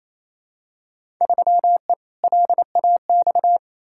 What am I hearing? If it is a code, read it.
Morse code, heard as a keyed tone, about 28 wpm, 700 Hz. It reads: 3E LAX
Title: Field Day Exchanges Encore - 28wpm